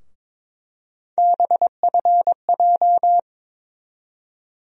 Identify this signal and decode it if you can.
Morse code, heard as a keyed tone, about 22 wpm, 700 Hz. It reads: BFJ
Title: USA Call Sign Postfixes - 3 Letters 22wpm